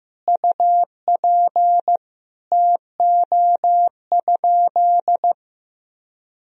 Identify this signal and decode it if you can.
Morse code, heard as a keyed tone, about 15 wpm, 700 Hz. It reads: UP TO?
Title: Sentences from Top 100 Words and Abbreviations 15wpm